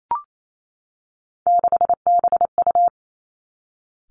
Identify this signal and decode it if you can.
Morse code, heard as a keyed tone, about 28 wpm, 700 Hz. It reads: 6BU
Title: Three Challenging Letter-Number Combinations 28wpm